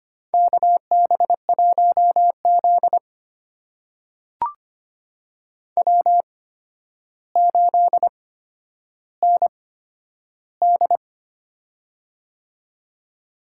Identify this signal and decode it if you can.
Morse code, heard as a keyed tone, about 25 wpm, spaced as for 11 wpm, 700 Hz. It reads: KB1Z W8ND
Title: USA Call Signs - 4 Characters - 8x Character Spacing - 25wpm